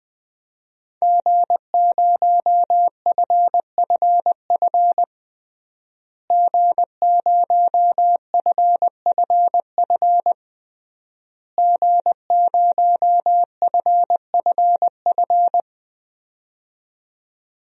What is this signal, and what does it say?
Morse code, heard as a keyed tone, about 20 wpm, 700 Hz. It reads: G0FFF G0FFF G0FFF
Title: DXCC-100 Call Signs - 5 Characters - Repeated 3x - 20wpm